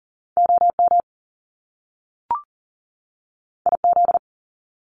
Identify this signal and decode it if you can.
Morse code, heard as a keyed tone, about 42 wpm, 700 Hz. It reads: OM IZ
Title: Top Two Letter Combinations in Words 40wpm